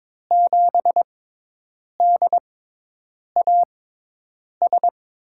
Morse code, heard as a keyed tone, about 22 wpm, spaced as for 8 wpm, 700 Hz. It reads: 7DAS
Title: USA Call Signs - 6x Character Spacing - 22wpm